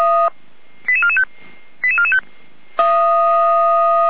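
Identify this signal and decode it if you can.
Five-tone selective call (ZVEI), heard as quick runs of five tones, tones of about 70 ms. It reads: 88445 88445
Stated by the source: ZVEI five-tone selective call 88505 received on SDR.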